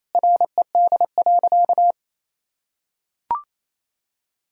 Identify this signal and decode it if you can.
Morse code, heard as a keyed tone, about 27 wpm, 700 Hz. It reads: RED.
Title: Sentences from Top 1000 Words - Review Part 2 - 28wpm